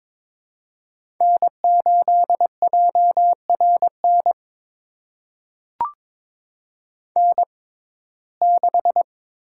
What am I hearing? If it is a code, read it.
Morse code, heard as a keyed tone, about 22 wpm, spaced as for 16 wpm, 700 Hz. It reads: N8JRN N6
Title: USA Call Signs - 5 Characters - 6x Character Spacing - 22wpm